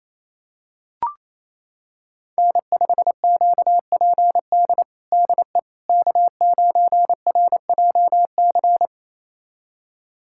Morse code, heard as a keyed tone, about 28 wpm, 700 Hz. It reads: N5QPD DE K9RJC
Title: USA Callsign DE Callsign - 5 Characters - 28wpm